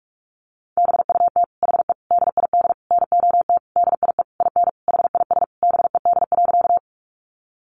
Morse code, heard as a keyed tone, about 44 wpm, 700 Hz. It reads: BUT HE DID NOT DIE IN HIS BED.
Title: Sentences from Top 1300 Words 45wpm